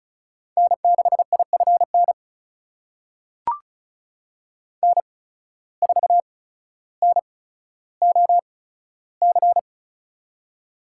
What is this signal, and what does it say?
Morse code, heard as a keyed tone, about 36 wpm, spaced as for 15 wpm, 700 Hz. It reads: N6IFN N4NOC
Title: USA Call Signs - 5 Characters - 8x Character Spacing - 35wpm